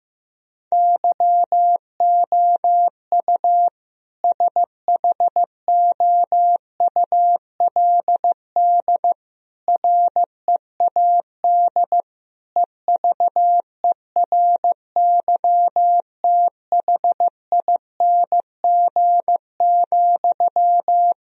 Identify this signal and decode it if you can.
Morse code, heard as a keyed tone, about 15 wpm, 700 Hz. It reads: YOU SHOULD READ EVERYTHING,
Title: Sentences from Top 1000 Words - Review Part 4 - 15wpm